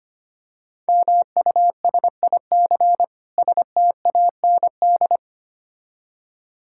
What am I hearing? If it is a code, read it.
Morse code, heard as a keyed tone, about 25 wpm, 700 Hz. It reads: MUSIC STAND
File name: practice / 25wpm / Difficult Words - Speed-Racing with Hints - 25wpm